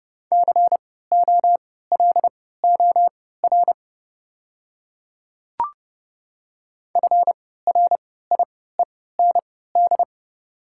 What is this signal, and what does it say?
Morse code, heard as a keyed tone, about 30 wpm, spaced as for 17 wpm, 700 Hz. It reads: COLOR FRIEND
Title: 3x Character Spacing - Top 300 Words - 30wpm